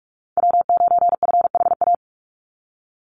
Morse code, heard as a keyed tone, about 44 wpm, 700 Hz. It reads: W9FHA
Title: NA SOTA Chaser Callsigns 45wpm